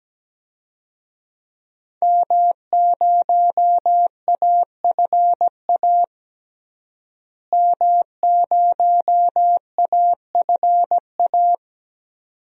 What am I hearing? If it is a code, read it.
Morse code, heard as a keyed tone, about 17 wpm, 700 Hz. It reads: M0AFA M0AFA
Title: DXCC-100 Call Signs - 5 Characters - Repeated 2x - 17wpm